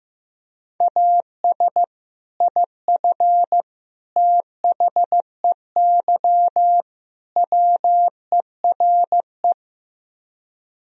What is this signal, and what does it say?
Morse code, heard as a keyed tone, about 15 wpm, 700 Hz. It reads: AS IF THEY WERE
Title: Speed-Racing - Sets of 4 Words 15wpm